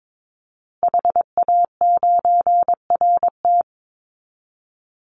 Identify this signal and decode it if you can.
Morse code, heard as a keyed tone, about 22 wpm, 700 Hz. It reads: HA9RT
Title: Call Signs Repeated 3x - 22wpm